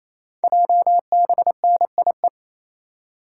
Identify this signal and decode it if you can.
Morse code, heard as a keyed tone, about 29 wpm, 700 Hz. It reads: JBNIE
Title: Code Groups - 5 Letters - 28wpm